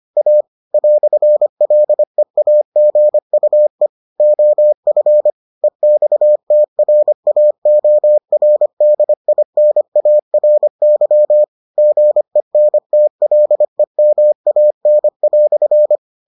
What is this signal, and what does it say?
Morse code, heard as a keyed tone, about 25 wpm, 600 Hz. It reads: A "LEAGUE OF EXTRAORDINARY GENTLEMAN"